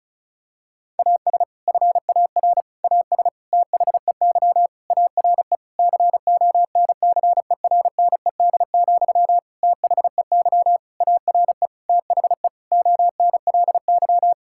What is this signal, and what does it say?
Morse code, heard as a keyed tone, about 36 wpm, 700 Hz. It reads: AS FAR AS THEY ARE CONCERNED, THEY ARE THE ONLY
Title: Sentences from Top 1000 Words - Review Part 6 - 35wpm